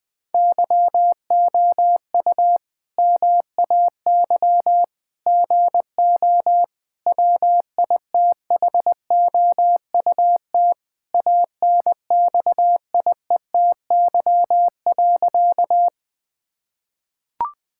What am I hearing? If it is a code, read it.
Morse code, heard as a keyed tone, about 20 wpm, 700 Hz. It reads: YOU MAY GO WITHOUT ANXIETY.
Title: Sentences from Top 1600 Words 20wpm